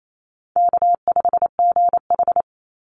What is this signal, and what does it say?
Morse code, heard as a keyed tone, about 28 wpm, 700 Hz. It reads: K5GH
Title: Easy Call Signs Repeated 3x - 28wpm